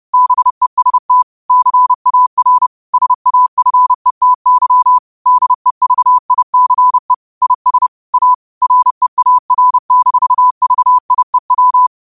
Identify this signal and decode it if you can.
Morse code, heard as a keyed tone, about 30 wpm, 1000 Hz. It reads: BEST CAR SAFETY DEVICE IS A REAR-VIEW
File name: qotdfiles / 30wpm